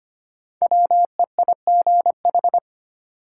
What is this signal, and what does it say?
Morse code, heard as a keyed tone, about 25 wpm, 700 Hz. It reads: WEIGH
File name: 5 Letter Words - First 3 Letters Given - 25wpm